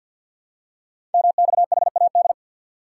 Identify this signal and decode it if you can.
Morse code, heard as a keyed tone, about 49 wpm, 700 Hz. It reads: MXHAD
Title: Code Groups - 5 Letters - 50wpm